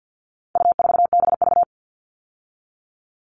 Morse code, heard as a keyed tone, about 50 wpm, 700 Hz. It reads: U4BV
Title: All Cut Numbers - 4-Digit 50wpm